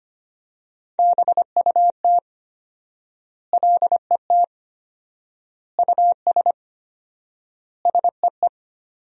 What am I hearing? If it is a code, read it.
Morse code, heard as a keyed tone, about 25 wpm, 700 Hz. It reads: BUT LET US SEE
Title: 4x Word-Spacing - Sentences from Top 500 Words - 25wpm